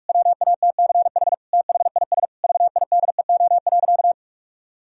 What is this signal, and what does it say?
Morse code, heard as a keyed tone, about 46 wpm, 700 Hz. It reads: WATCH THIS VIDEO.
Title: Sentences from Top 1000 Words - Review Part 4 - 45wpm